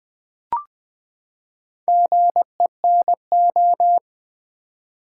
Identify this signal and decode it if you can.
Morse code, heard as a keyed tone, about 20 wpm, 700 Hz. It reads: GENO
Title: Build A Word - 4 Letters At A Time - 8-Letter Words - 20wpm